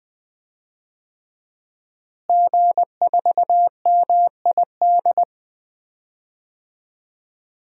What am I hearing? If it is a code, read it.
Morse code, heard as a keyed tone, about 20 wpm, 700 Hz. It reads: G4MID
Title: DXCC-100 Call Signs - 5 Characters - Repeated 3x - 20wpm